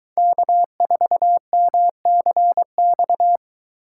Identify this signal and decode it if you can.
Morse code, heard as a keyed tone, about 22 wpm, 700 Hz. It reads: K4MCX